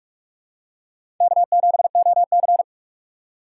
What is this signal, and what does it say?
Morse code, heard as a keyed tone, about 41 wpm, 700 Hz. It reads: K7OC
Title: USA Call Signs - 6x Character Spacing - 45wpm